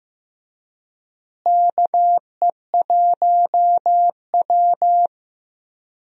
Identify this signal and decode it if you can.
Morse code, heard as a keyed tone, about 15 wpm, 700 Hz. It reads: KE1W
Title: USA Call Signs - 4 Characters - Repeated 3x - 15wpm